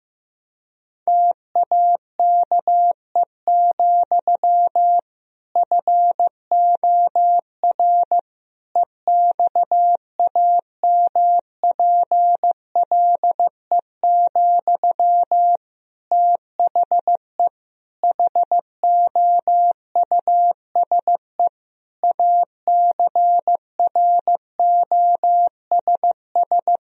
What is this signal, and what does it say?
Morse code, heard as a keyed tone, about 15 wpm, 700 Hz. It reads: TAKE, FOR EXAMPLE, THE HOUSE ACROSS
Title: Sentences from Top 1000 Words - Review Part 1 - 15wpm